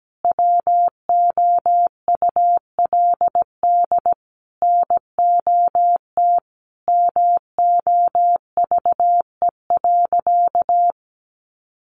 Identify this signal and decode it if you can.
Morse code, heard as a keyed tone, about 17 wpm, 700 Hz. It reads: WOULD NOT MOVE.
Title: Sentences from Top 600 Words 17wpm